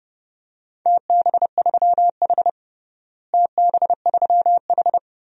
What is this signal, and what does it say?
Morse code, heard as a keyed tone, about 31 wpm, 700 Hz. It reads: TB3H TB3H
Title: DXCC-100 Call Signs - 4 Characters - Repeated 3x - 30wpm